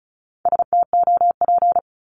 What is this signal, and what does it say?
Morse code, heard as a keyed tone, about 36 wpm, 700 Hz. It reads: STOP
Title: Multi-letter Phonemes 35wpm